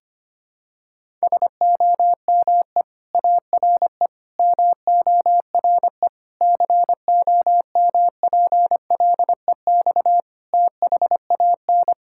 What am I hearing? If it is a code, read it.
Morse code, heard as a keyed tone, about 25 wpm, 700 Hz. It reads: SOME ARE MORE COMPLEX THAN
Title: Sentences from Top 1000 Words - Review Part 6 - 25wpm